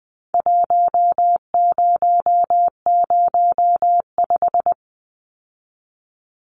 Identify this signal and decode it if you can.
Morse code, heard as a keyed tone, about 20 wpm, 700 Hz. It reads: 1005
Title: Numbers - Contest Style - 4-Digit 20wpm